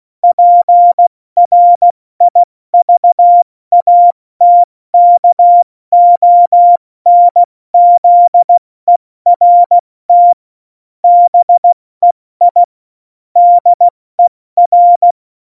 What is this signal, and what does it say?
Morse code, heard as a keyed tone, about 16 wpm, 700 Hz. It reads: PRIVATKONZERT BEI DER
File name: Morsekurs / Witze / Witz 01. Dezember 2020, 70 F80